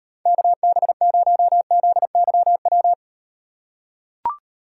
Morse code, heard as a keyed tone, about 38 wpm, 700 Hz. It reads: KB0ZYW